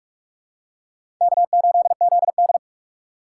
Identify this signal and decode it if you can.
Morse code, heard as a keyed tone, about 45 wpm, 700 Hz. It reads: K8ZD
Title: USA Call Signs - 4x Character Spacing - 45wpm